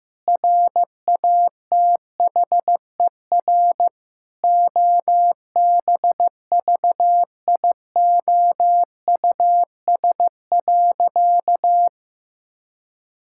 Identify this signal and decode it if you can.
Morse code, heard as a keyed tone, about 15 wpm, 700 Hz. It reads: RATHER OBVIOUS.
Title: Sentences from Top 1600 Words 15wpm